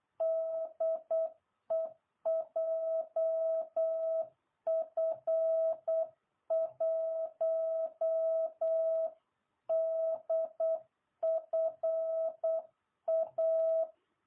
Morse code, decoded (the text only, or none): DE JF1DFA